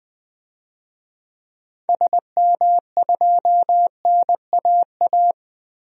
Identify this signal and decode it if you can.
Morse code, heard as a keyed tone, about 20 wpm, 700 Hz. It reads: SM2NAA
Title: DXCC-100 Call Signs - 6 Characters - Repeated 3x - 20wpm